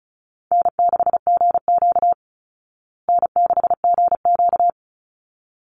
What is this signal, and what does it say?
Morse code, heard as a keyed tone, about 34 wpm, 700 Hz. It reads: N6GQ N6GQ
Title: Easy Call Signs Repeated 2x - 35wpm